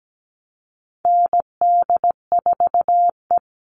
Morse code, heard as a keyed tone, about 17 wpm, 700 Hz. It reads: ND4E
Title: All Cut Numbers - 4-Digit 17wpm